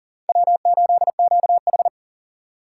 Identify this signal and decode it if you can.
Morse code, heard as a keyed tone, about 39 wpm, 700 Hz. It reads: W8QH
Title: USA Call Signs - 4 Characters - 3x Character Spacing - 40wpm